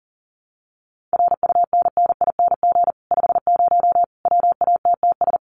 Morse code, heard as a keyed tone, about 40 wpm, 700 Hz. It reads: RUNNING 50 WATTS
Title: Common QSO Phrases 40wpm